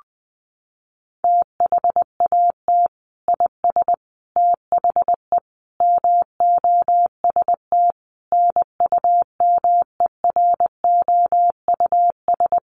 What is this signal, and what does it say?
Morse code, heard as a keyed tone, about 20 wpm, 700 Hz. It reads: THAT IS THE MOST NUMEROUS
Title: Sentences from Top 1400 Words 20wpm